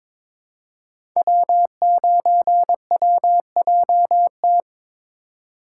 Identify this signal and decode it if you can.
Morse code, heard as a keyed tone, about 22 wpm, 700 Hz. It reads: W9WJT
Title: Code Groups - 5 Letter:Numbers - 22wpm